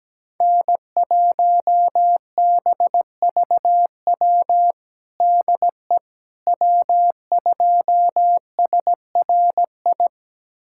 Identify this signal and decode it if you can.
Morse code, heard as a keyed tone, about 17 wpm, 700 Hz. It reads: N1BVW DE W2SRI
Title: USA Callsign DE Callsign - 5 Characters - 17wpm